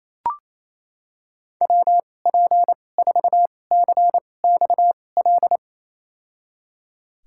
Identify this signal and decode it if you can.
Morse code, heard as a keyed tone, about 28 wpm, spaced as for 20 wpm, 700 Hz. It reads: WP4CXL
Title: USA Call Signs - 6 Characters - 2x Character Spacing - 28wpm